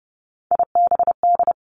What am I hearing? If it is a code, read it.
Morse code, heard as a keyed tone, about 30 wpm, 700 Hz. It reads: IBD
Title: Three Challenging Letter-Number Combinations 30wpm